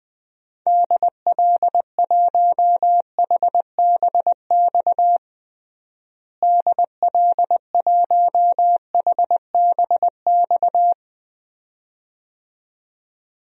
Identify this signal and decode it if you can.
Morse code, heard as a keyed tone, about 20 wpm, 700 Hz. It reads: DL1HBX DL1HBX
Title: DXCC-100 Call Signs - 6 Characters - Repeated 2x - 20wpm